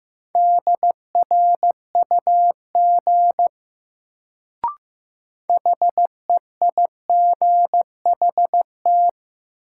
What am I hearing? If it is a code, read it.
Morse code, heard as a keyed tone, about 15 wpm, 700 Hz. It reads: DRUG HEIGHT
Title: Sets of 5 Words - New Word Drill - 15wpm